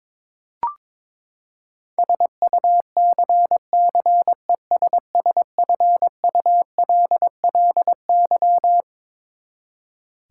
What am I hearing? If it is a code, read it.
Morse code, heard as a keyed tone, about 21 wpm, 700 Hz. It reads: SUCCESSFULLY
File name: Twelve Letter Words - 22wpm